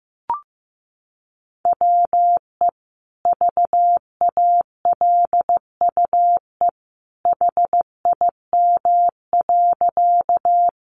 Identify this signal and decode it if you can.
Morse code, heard as a keyed tone, about 15 wpm, 700 Hz. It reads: WE VALUE HIM.
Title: Sentences from Top 600 Words 15wpm